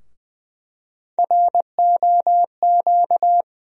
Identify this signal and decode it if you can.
Morse code, heard as a keyed tone, about 20 wpm, 700 Hz. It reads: ROQ
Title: USA Call Sign Postfixes - 3 Letters 20wpm